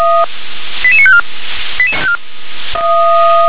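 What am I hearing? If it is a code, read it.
Five-tone selective call (ZVEI), heard as quick runs of five tones, tones of about 70 ms. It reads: 88954
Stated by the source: ZVEI five-tone selective call received on an SDR.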